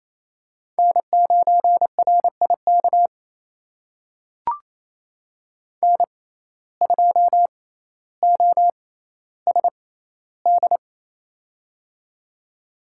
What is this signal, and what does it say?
Morse code, heard as a keyed tone, about 28 wpm, spaced as for 14 wpm, 700 Hz. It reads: N9RIK N2OSD
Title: USA Call Signs - 5 Characters - 6x Character Spacing - 28wpm